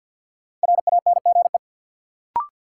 Morse code, heard as a keyed tone, about 53 wpm, 700 Hz. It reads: RANGE
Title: Sets of 4 Words - New Word Drill - 50wpm